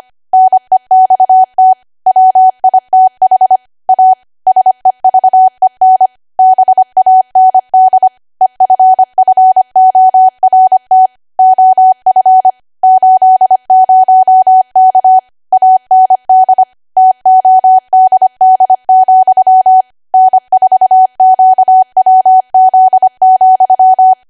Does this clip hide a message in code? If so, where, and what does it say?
Yes: Morse code from 0.3 to 24.1 s, heard as a keyed tone, about 25 wpm, 750 Hz. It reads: NEXT WITH A SEVEN BAND EFFORT OF 80K AND TODD, N4QWZ,